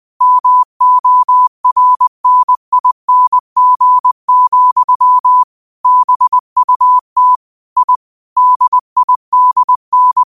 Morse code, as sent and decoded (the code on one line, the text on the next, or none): -- --- .-. -. .. -. --. --..-- / -... ..- - / .. / -.. .. -.. -.
MORNING, BUT I DIDN